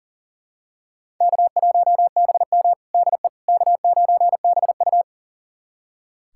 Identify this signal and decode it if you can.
Morse code, heard as a keyed tone, about 37 wpm, 700 Hz. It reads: K1BM DE K9BU
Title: USA Callsign DE Callsign - 4 Characters - 40wpm